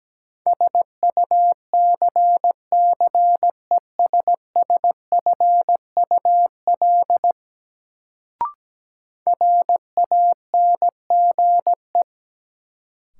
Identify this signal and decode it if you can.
Morse code, heard as a keyed tone, about 17 wpm, 700 Hz. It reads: SUCCESSFUL RANGE
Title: Sentences from Top 900 Words - New Word Drill - 17wpm